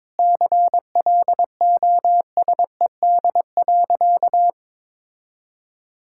Morse code, heard as a keyed tone, about 22 wpm, 700 Hz. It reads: CLOSED.
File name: Sentences with 7-Word Phrases - 22wpm